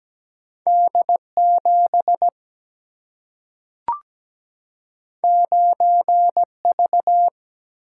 Morse code, heard as a keyed tone, about 17 wpm, 700 Hz. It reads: D7 9V
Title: Single-Letter Single-Number 17wpm